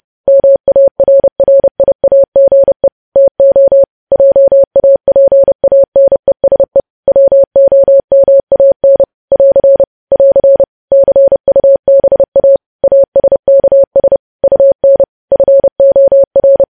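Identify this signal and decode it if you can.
Morse code, heard as a keyed tone, about 30 wpm, 550 Hz. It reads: MARRIAGE TO JAPANESE WOMAN <AR> <AR> CUBA ASKS UN FOR
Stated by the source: Morse code sending of news headlines at 30 wpm.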